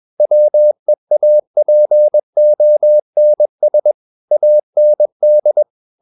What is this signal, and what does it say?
Morse code, heard as a keyed tone, about 21 wpm, 600 Hz. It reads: WEAPONS AND
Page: CW med Gnister